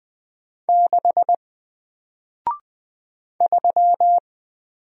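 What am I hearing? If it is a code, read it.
Morse code, heard as a keyed tone, about 20 wpm, 700 Hz. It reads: 6 3
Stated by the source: Single letter and number Morse drill at 20 wpm, from N4LCD.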